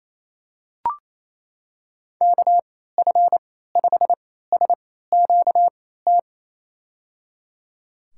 Morse code, heard as a keyed tone, about 28 wpm, spaced as for 16 wpm, 700 Hz. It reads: KF5SQT
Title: USA Call Signs - 6 Characters - 3x Character Spacing - 28wpm